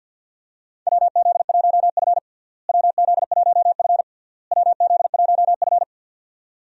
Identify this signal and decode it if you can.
Morse code, heard as a keyed tone, about 49 wpm, 700 Hz. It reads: WZ1F WZ1F WZ1F
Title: USA Call Signs - 4 Characters - Repeated 3x - 50wpm